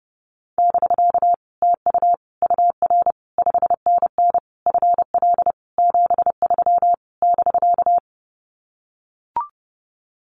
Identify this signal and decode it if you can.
Morse code, heard as a keyed tone, about 28 wpm, 700 Hz. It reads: <BK> TU UR 5NN FL 73 <BK>
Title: Common QSO Phrases Encore 30wpm